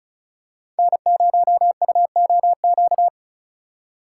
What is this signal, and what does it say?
Morse code, heard as a keyed tone, about 36 wpm, 700 Hz. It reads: N0UOQ
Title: USA Call Signs - 8x Character Spacing - 35wpm